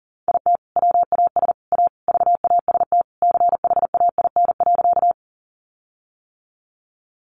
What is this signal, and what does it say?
Morse code, heard as a keyed tone, about 40 wpm, 700 Hz. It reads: IT WAS A VAST CHAIN.
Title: Sentences from Top 1500 Words 40wpm